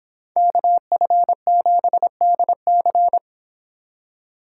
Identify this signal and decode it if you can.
Morse code, heard as a keyed tone, about 26 wpm, 700 Hz. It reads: KF7DC